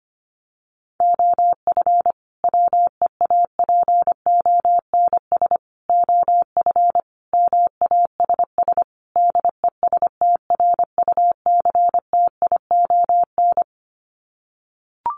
Morse code, heard as a keyed tone, about 25 wpm, 700 Hz. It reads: OF WEAPONS OF MASS DESTRUCTION
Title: Sets of 5 Words Encore 25wpm